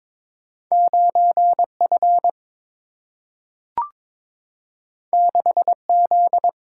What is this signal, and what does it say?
Morse code, heard as a keyed tone, about 22 wpm, 700 Hz. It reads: 9F 6Z
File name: practice / 22wpm / Two Letter-Number Combinations - 22wpm